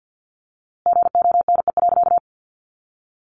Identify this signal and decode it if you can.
Morse code, heard as a keyed tone, about 50 wpm, 700 Hz. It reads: GONE.
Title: A Sound Of Thunder 50wpm